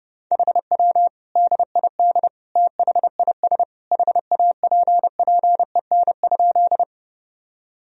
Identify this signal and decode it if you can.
Morse code, heard as a keyed tone, about 30 wpm, 700 Hz. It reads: HW DID THIS HAPPEN?
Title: Sentences from Top 400 Words and Abbreviations 30wpm